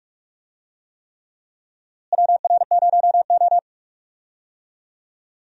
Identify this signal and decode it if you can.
Morse code, heard as a keyed tone, about 46 wpm, 700 Hz. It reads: WR0O
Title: USA Call Signs - 4 Characters - Repeated 3x - 45wpm